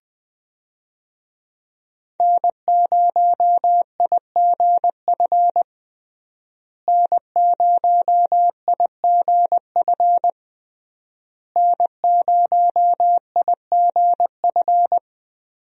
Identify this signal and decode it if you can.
Morse code, heard as a keyed tone, about 20 wpm, 700 Hz. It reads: N0IGF N0IGF N0IGF
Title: USA Call Signs - 5 Characters - Repeated 3x - 20wpm